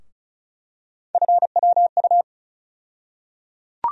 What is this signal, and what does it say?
Morse code, heard as a keyed tone, about 36 wpm, 700 Hz. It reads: FWU
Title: USA Call Sign Postfixes - 3 Letters 35wpm